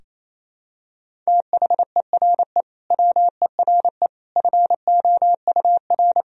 Morse code, heard as a keyed tone, about 28 wpm, 700 Hz. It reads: THERE WERE FOUR
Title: Sentences from Top 1000 Words - Review Part 1 - 28wpm